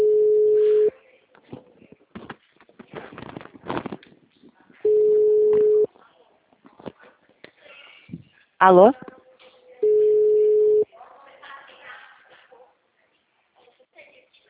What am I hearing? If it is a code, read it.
Ringback tone (Europe, 425 Hz), with speech.
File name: out- -2034- - - .19569